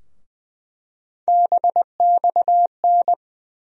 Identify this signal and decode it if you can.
Morse code, heard as a keyed tone, about 20 wpm, 700 Hz. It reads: BXN
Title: USA Call Sign Postfixes - 3 Letters 20wpm